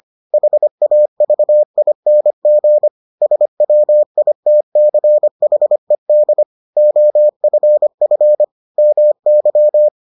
Morse code, heard as a keyed tone, about 25 wpm, 600 Hz. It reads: HAVING SWITCHED OFF MY